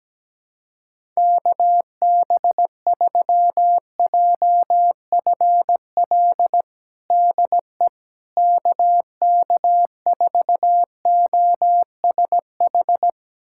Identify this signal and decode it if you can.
Morse code, heard as a keyed tone, about 17 wpm, 700 Hz. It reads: KB3JFL DE KK4OSH
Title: USA Callsign DE Callsign 17wpm